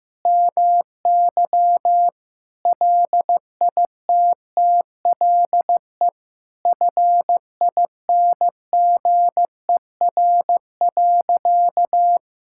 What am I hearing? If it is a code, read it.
Morse code, heard as a keyed tone, about 15 wpm, 700 Hz. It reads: MY LITTLE FINGER.
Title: Sentences from Top 1600 Words 15wpm